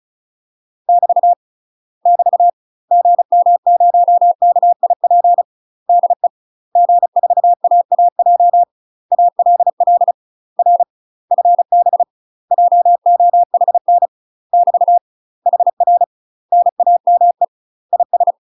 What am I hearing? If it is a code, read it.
Morse code, heard as a keyed tone, about 34 wpm, 700 Hz. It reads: <BT> <BT> GM0KIP DE G4AAJ ALL R FB JOHN <BT> HR NAME IS
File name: UK MASTER QSOs TRAINING/35WPM